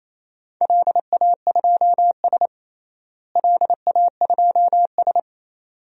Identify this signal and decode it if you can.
Morse code, heard as a keyed tone, about 28 wpm, 700 Hz. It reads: LA2S LA2S
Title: DXCC-100 Call Signs - 4 Characters - Repeated 2x - 28wpm